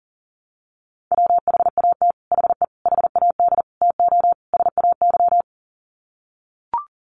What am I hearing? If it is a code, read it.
Morse code, heard as a keyed tone, about 40 wpm, 700 Hz. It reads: WHAT HE HAD TO SAY
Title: Sets of 5 Words Encore2 40wpm